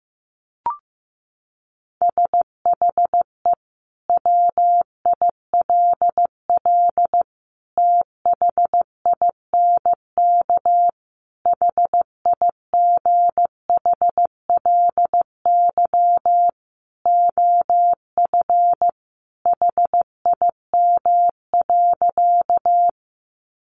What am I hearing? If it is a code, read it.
Morse code, heard as a keyed tone, about 15 wpm, 700 Hz. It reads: SHE WILL THINK HIGHLY OF HIM.
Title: Sentences from Top 1100 Words 15wpm